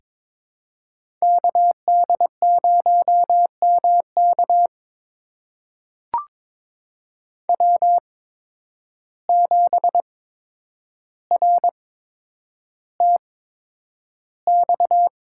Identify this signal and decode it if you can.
Morse code, heard as a keyed tone, about 22 wpm, spaced as for 9 wpm, 700 Hz. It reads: KD0MK W7RTX
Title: USA Call Signs - 5 Characters - 8x Character Spacing - 22wpm